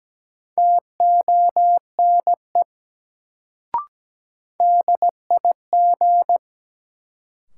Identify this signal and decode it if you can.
Morse code, heard as a keyed tone, about 17 wpm, 700 Hz. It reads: TONE DIG
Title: Sentences from Top 1600 Words - New Word Drill Alt - 17wpm